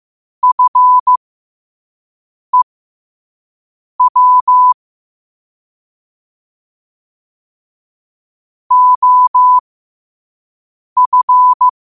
Morse code, heard as a keyed tone, about 15 wpm, spaced as for 5 wpm, 1000 Hz. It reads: FEW OF
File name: qotdfiles / 5wpm